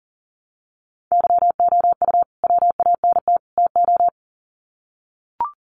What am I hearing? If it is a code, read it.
Morse code, heard as a keyed tone, about 40 wpm, 700 Hz. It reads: YOU WANT TO
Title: Sets of 3 Words 40wpm